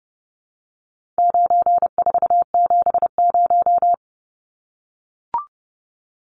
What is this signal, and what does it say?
Morse code, heard as a keyed tone, about 29 wpm, 700 Hz. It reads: 9470
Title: Numbers - 4-Digit 30wpm